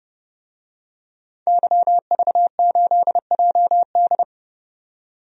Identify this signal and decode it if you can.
Morse code, heard as a keyed tone, about 30 wpm, 700 Hz. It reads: YV8JD
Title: DXCC-100 Call Signs - 5 Characters - 30wpm